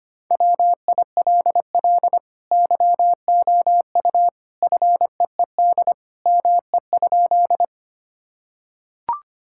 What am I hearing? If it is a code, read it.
Morse code, heard as a keyed tone, about 25 wpm, 700 Hz. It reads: WILL YOU FEED ME?
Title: Sentences from Top 1600 Words 25wpm